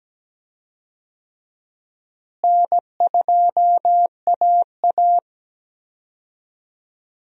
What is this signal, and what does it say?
Morse code, heard as a keyed tone, about 17 wpm, 700 Hz. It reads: N2AA
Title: USA Call Signs - 4 Characters - Repeated 3x - 17wpm